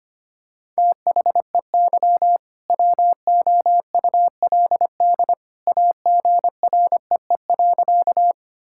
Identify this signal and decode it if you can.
Morse code, heard as a keyed tone, about 25 wpm, 700 Hz. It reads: THEY WOULD AGREE.
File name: practice / 25wpm / Sentences with 2-Word Phrases - 25wpm